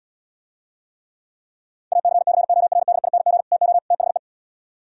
Morse code, heard as a keyed tone, about 77 wpm, 700 Hz. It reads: ACCORDING TO THE